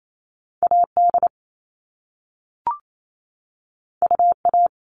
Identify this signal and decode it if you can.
Morse code, heard as a keyed tone, about 28 wpm, 700 Hz. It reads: AD UA
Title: All Cut Numbers - 2-Digit 28wpm